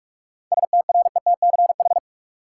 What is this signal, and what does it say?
Morse code, heard as a keyed tone, about 46 wpm, 700 Hz. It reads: STRETCH
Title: Seven Letter Words - 45wpm